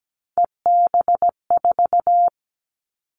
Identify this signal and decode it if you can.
Morse code, heard as a keyed tone, about 17 wpm, 700 Hz. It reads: EB4
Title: Callsign Prefixes for Top 25 DXCC - Repeated 3x - 17wpm